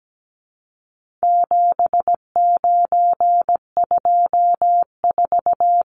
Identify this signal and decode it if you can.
Morse code, heard as a keyed tone, about 17 wpm, 700 Hz. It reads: 7924
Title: Numbers - 4-Digit 17wpm